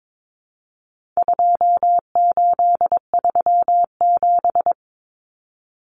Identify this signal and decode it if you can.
Morse code, heard as a keyed tone, about 22 wpm, 700 Hz. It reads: 2837
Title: Common Cut Numbers - 4-Digit 22wpm